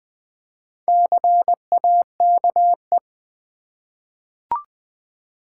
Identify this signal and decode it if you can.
Morse code, heard as a keyed tone, about 20 wpm, 700 Hz. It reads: CAKE
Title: Four Letter Words - 20wpm